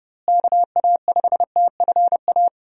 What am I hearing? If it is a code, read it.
Morse code, heard as a keyed tone, about 30 wpm, 700 Hz. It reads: KA5TFA
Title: USA Call Signs - 6 Characters - Repeated 2x - 30wpm